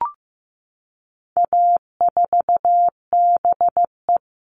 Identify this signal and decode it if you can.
Morse code, heard as a keyed tone, about 15 wpm, 700 Hz. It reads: A4BE
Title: All Cut Numbers - 4-Digit 15wpm